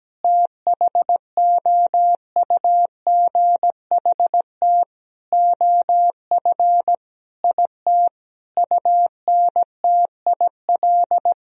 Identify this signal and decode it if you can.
Morse code, heard as a keyed tone, about 17 wpm, 700 Hz. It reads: THOUGHT OF IT UNTIL
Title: Sentences from Top 1000 Words - Review Part 1 - 17wpm